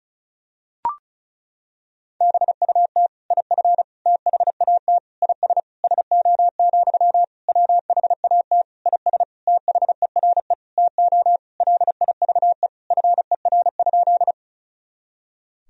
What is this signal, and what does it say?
Morse code, heard as a keyed tone, about 34 wpm, 690 Hz. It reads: BUT IF THAT IS SO, WHAT IS THERE TO LIVE FER?
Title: Sentences from Top 200 Words and Abbreviations 35wpm